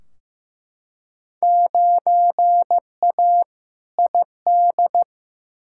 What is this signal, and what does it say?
Morse code, heard as a keyed tone, about 15 wpm, 700 Hz. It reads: 9A ID
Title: Field Day Exchanges Encore - 15wpm